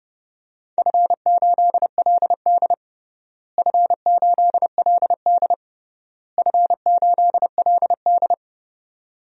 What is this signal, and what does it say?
Morse code, heard as a keyed tone, about 30 wpm, 700 Hz. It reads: F8LD F8LD F8LD
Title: DXCC-100 Call Signs - 4 Characters - Repeated 3x - 30wpm